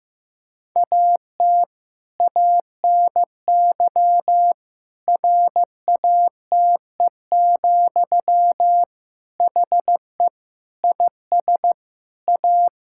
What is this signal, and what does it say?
Morse code, heard as a keyed tone, about 15 wpm, 700 Hz. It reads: AT ANY RATE, HE IS A
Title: Sentences from Top 1000 Words - Review Part 5 - 15wpm